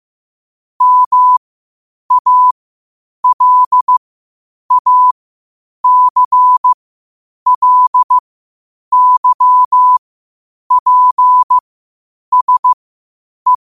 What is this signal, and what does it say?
Morse code, heard as a keyed tone, about 15 wpm, spaced as for 8 wpm, 1000 Hz. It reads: MALACLYPSE
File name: qotdfiles / 8wpm